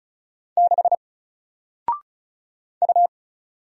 Morse code, heard as a keyed tone, about 36 wpm, 700 Hz. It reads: 6 U
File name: Single-Letter-Number-35wpm